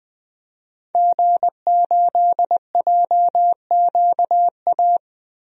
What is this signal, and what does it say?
Morse code, heard as a keyed tone, about 19 wpm, 700 Hz. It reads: G8JQA
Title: DXCC-100 Call Signs - 5 Characters - Repeated 3x - 20wpm